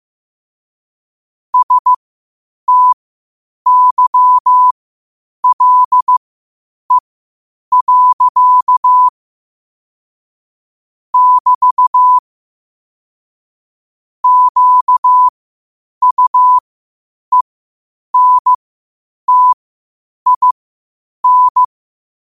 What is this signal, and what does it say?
Morse code, heard as a keyed tone, about 15 wpm, spaced as for 8 wpm, 1000 Hz. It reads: STYLE. <BT> QUENTIN
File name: qotdfiles / 8wpm